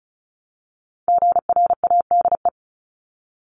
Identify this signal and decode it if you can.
Morse code, heard as a keyed tone, about 34 wpm, 700 Hz. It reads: GRADE
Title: Top 1501-1750 Words 35wpm